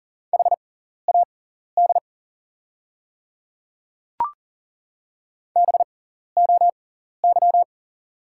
Morse code, heard as a keyed tone, about 40 wpm, spaced as for 14 wpm, 700 Hz. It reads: HAD BOY
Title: 6x Character Spacing - Top 200 Words - 40wpm